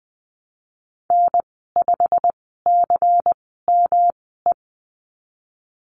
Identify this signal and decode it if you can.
Morse code, heard as a keyed tone, about 20 wpm, spaced as for 15 wpm, 700 Hz. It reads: N5CME
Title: 140 Lesson 23g - Letter f - Call Signs 20wpm